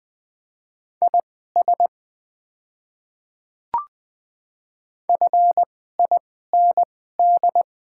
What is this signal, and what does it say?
Morse code, heard as a keyed tone, about 20 wpm, spaced as for 15 wpm, 700 Hz. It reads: IS FIND
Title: 2x Character Spacing - Top 100 Words - 20wpm